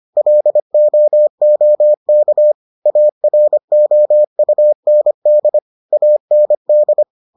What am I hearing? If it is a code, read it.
Morse code, heard as a keyed tone, about 25 wpm, 600 Hz. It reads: LOOK AROUND AND